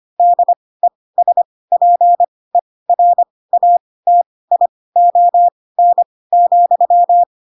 Morse code, heard as a keyed tone, about 25 wpm, spaced as for 18 wpm, 700 Hz. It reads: DESPERATION,
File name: Pinocchio / Pinocchio ch6 12 25 ik1hge